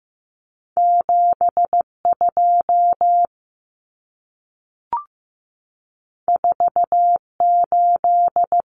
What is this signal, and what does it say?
Morse code, heard as a keyed tone, about 15 wpm, 700 Hz. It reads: 72 48
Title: Numbers - 2-Digit 15wpm